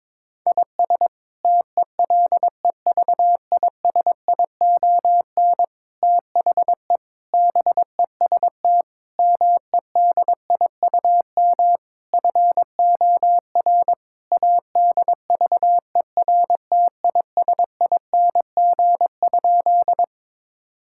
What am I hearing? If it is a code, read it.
Morse code, heard as a keyed tone, about 22 wpm, 700 Hz. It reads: IS TELEVISION THE BEST MEDIUM FOR ADVERTISING?
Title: Sentences from Top 1000 Words - Review Part 6 - 22wpm